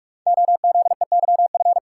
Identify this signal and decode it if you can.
Morse code, heard as a keyed tone, about 44 wpm, 700 Hz. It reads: OZEYF
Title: Code Groups - 5 Letter:Numbers - 45wpm